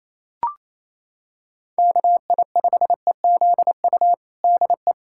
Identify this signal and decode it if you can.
Morse code, heard as a keyed tone, about 27 wpm, 700 Hz. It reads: KI5EZU DE
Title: USA Callsign DE Callsign - 6 Characters - 28wpm